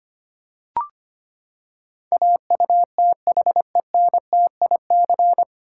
Morse code, heard as a keyed tone, about 25 wpm, 700 Hz. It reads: AUTHENTIC
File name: Nine Letter Words - 25wpm